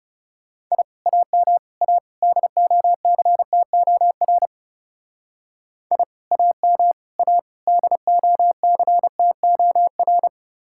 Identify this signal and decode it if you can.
Morse code, heard as a keyed tone, about 38 wpm, 700 Hz. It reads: I AM A DOCTOR I AM A DOCTOR